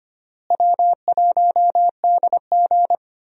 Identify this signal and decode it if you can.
Morse code, heard as a keyed tone, about 25 wpm, 700 Hz. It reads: W1DG
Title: USA Call Signs - 4 Characters - Repeated 3x - 25wpm